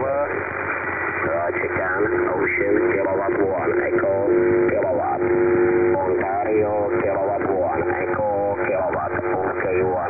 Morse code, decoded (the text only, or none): none